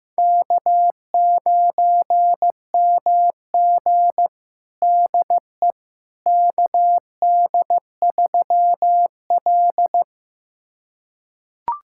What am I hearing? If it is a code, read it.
Morse code, heard as a keyed tone, about 15 wpm, 700 Hz. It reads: K9MG DE KD3L
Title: USA Callsign DE Callsign - 4 Characters - 15wpm